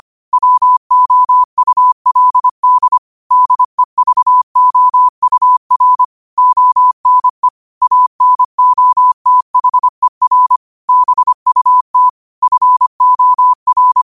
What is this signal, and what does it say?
Morse code, heard as a keyed tone, about 25 wpm, 1000 Hz. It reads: WOULD DEVOUR ONE ANOTHER BUT FOR
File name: qotdfiles / 25wpm